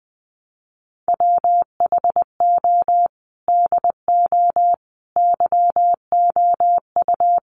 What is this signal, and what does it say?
Morse code, heard as a keyed tone, about 20 wpm, 700 Hz. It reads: WHO DO YOU
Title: Sentences from Top 1100 Words 20wpm